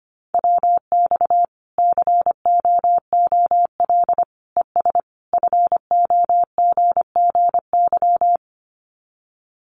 Morse code, heard as a keyed tone, about 25 wpm, 700 Hz. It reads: WX COOL ES FOGGY
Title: Common QSO Phrases 25wpm